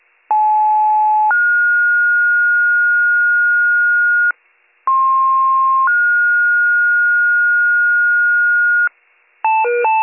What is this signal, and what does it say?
Two-tone page, heard as a one-second tone, then a three-second tone, about 860 then 1430 Hz, about 1060 then 1430 Hz. It is heard from 0.3 to 8.9 s.